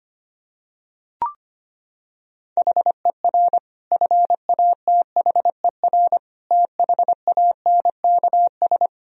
Morse code, heard as a keyed tone, about 25 wpm, 700 Hz. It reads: HER FATHER THANKS
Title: Sentences from Top 1000 Words - Review Part 3 - 25wpm